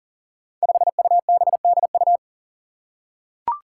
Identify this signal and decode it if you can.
Morse code, heard as a keyed tone, about 40 wpm, 700 Hz. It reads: 5UBDU
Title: Five Challenging Letter-Number Combinations 40wpm